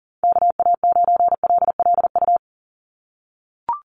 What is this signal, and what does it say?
Morse code, heard as a keyed tone, about 40 wpm, 700 Hz. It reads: KA9LLU
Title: USA Call Signs 40wpm